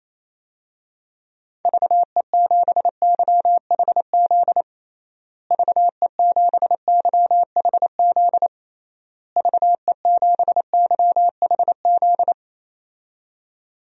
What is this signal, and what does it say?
Morse code, heard as a keyed tone, about 28 wpm, 700 Hz. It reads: VE7YHZ VE7YHZ VE7YHZ
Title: DXCC-100 Call Signs - 6 Characters - Repeated 3x - 28wpm